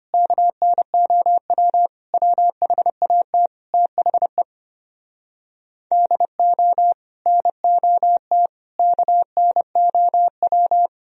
Morse code, heard as a keyed tone, about 33 wpm, 700 Hz. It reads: KNOW WHAT THE DO NOT KNOW